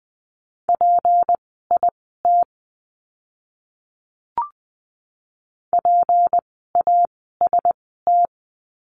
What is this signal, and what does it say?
Morse code, heard as a keyed tone, about 20 wpm, spaced as for 15 wpm, 700 Hz. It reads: PIT PAST
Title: 153 Lesson 25f - Letter p - Words 20wpm